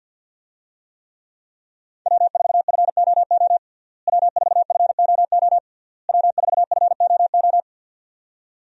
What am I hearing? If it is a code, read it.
Morse code, heard as a keyed tone, about 50 wpm, 700 Hz. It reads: W4FOO W4FOO W4FOO
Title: USA Call Signs - 5 Characters - Repeated 3x - 50wpm